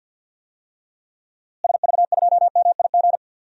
Morse code, heard as a keyed tone, about 52 wpm, 700 Hz. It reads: SV2MIG
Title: DXCC-100 Call Signs - 6 Characters - Repeated 3x - 50wpm